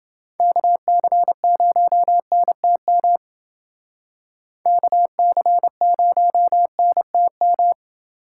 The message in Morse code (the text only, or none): KC0NTM KC0NTM